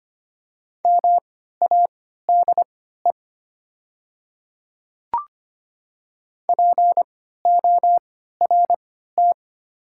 Morse code, heard as a keyed tone, about 25 wpm, spaced as for 14 wpm, 700 Hz. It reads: MADE PORT
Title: 3x Character Spacing - Top 500 Words - 25wpm